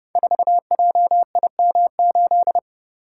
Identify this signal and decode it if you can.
Morse code, heard as a keyed tone, about 30 wpm, 700 Hz. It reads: 4JIM8
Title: Code Groups - 5 Letter:Numbers - 30wpm